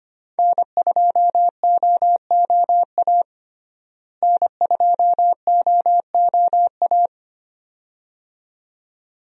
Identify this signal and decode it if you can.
Morse code, heard as a keyed tone, about 25 wpm, 700 Hz. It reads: N2OOA N2OOA
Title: USA Call Signs - 5 Characters - Repeated 2x - 25wpm